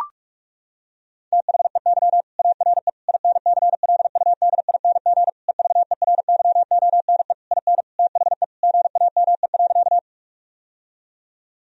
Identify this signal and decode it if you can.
Morse code, heard as a keyed tone, about 41 wpm, 700 Hz. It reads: THEY ARE INCLUDING EVERYONE IN THE GAME.
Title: Sentences from Top 1000 Words - Review Part 4 - 45wpm